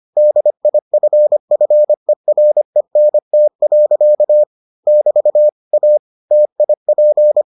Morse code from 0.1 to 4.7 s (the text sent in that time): DIFFERENT.